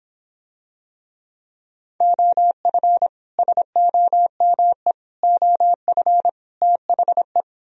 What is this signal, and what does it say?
Morse code, heard as a keyed tone, about 26 wpm, 700 Hz. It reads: OF SOME OF THE